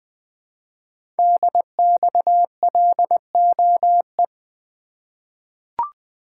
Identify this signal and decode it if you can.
Morse code, heard as a keyed tone, about 20 wpm, 700 Hz. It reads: DXLOE
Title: Code Groups - 5 Letters - 20wpm